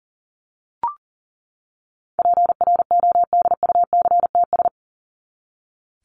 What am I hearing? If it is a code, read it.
Morse code, heard as a keyed tone, about 40 wpm, 700 Hz. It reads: PRODUCTS
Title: ES:S Suffix - Top 100 Words with Base Words 40wpm